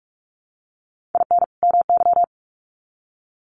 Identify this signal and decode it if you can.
Morse code, heard as a keyed tone, about 44 wpm, 700 Hz. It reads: IN MY
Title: Sets of 2 Words 45wpm